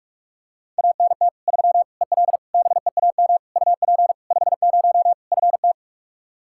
Morse code, heard as a keyed tone, about 46 wpm, 700 Hz. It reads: ANT 3 EL BEAM UP 50 FT
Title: Practice QSO - Exchange 2 - Info - 45wpm